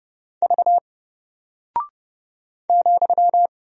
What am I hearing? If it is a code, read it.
Morse code, heard as a keyed tone, about 30 wpm, 700 Hz. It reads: V ,
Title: Single Character (Letter:Number:Punctuation:Prosign) 30wpm